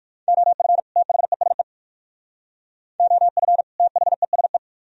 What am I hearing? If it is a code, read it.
Morse code, heard as a keyed tone, about 53 wpm, 700 Hz. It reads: OF THESE OF THESE
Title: Speed-Racing - Sets of 2 Words 45wpm